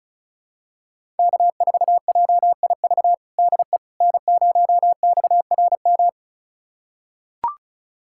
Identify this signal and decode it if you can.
Morse code, heard as a keyed tone, about 36 wpm, 700 Hz. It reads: K4JIV DE N0XRM
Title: USA Callsign DE Callsign - 5 Characters - 35wpm